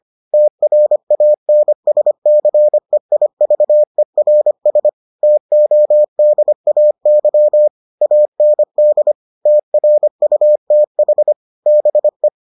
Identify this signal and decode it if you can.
Morse code, heard as a keyed tone, about 25 wpm, 600 Hz. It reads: TRANSCEIVERS TODAY AND TRUTH BE